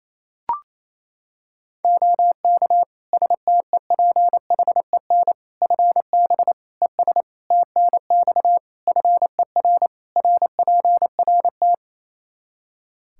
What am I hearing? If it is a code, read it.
Morse code, heard as a keyed tone, about 28 wpm, 700 Hz. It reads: OK STEPHEN FB ES TNX FER RPRT
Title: Practice QSO - Exchange 2 - Info - 28wpm